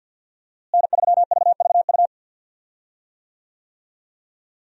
Morse code, heard as a keyed tone, about 51 wpm, 700 Hz. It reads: N3VVU
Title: USA Call Signs - 5 Characters - Repeated 3x - 50wpm